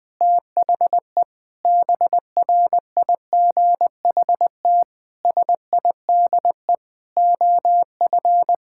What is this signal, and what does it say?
Morse code, heard as a keyed tone, about 20 wpm, 700 Hz. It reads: THE BRIGHT SIDE OF
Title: Sentences with 8-Word Phrases - 20wpm